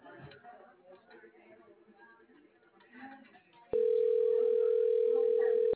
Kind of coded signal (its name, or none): telephone call progress tone